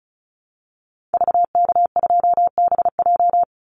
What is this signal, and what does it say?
Morse code, heard as a keyed tone, about 34 wpm, 700 Hz. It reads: VK2BJ
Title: Call Signs Repeated 2x - 35wpm